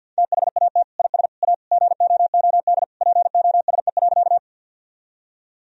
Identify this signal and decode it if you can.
Morse code, heard as a keyed tone, about 50 wpm, 700 Hz. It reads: THAT IS A GOOD POSE.
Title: Sentences from Top 1000 Words - Review Part 2 - 50wpm